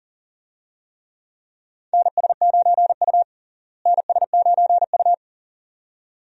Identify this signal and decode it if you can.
Morse code, heard as a keyed tone, about 40 wpm, 700 Hz. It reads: NS9U NS9U
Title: USA Call Signs - 4 Characters - Repeated 2x - 40wpm